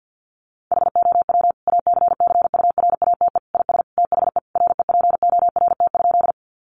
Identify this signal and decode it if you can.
Morse code, heard as a keyed tone, about 50 wpm, 700 Hz. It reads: HOW ACCURATE IS THE REPORT?
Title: Sentences from Top 1200 Words 50wpm